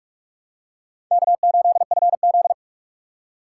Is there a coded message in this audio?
Yes: Morse code from 1.1 to 2.5 s, heard as a keyed tone, about 45 wpm, 700 Hz. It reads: K8FZ